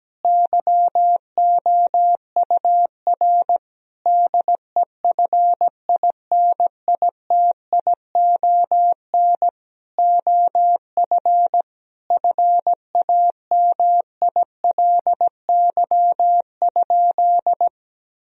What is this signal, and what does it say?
Morse code, heard as a keyed tone, about 17 wpm, 700 Hz. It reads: YOUR DEFINITION OF FAMILY?
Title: Sentences from Top 1000 Words - Review Part 4 - 17wpm